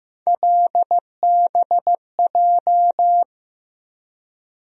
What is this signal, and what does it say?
Morse code, heard as a keyed tone, about 15 wpm, 700 Hz. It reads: LBJ
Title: USA Call Sign Postfixes - 3 Letters 15wpm